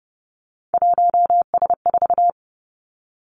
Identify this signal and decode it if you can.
Morse code, heard as a keyed tone, about 30 wpm, 700 Hz. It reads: 1S4
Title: Three Challenging Letter-Number Combinations 30wpm